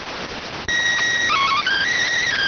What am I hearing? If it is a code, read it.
SSTV (slow-scan television), PD 290.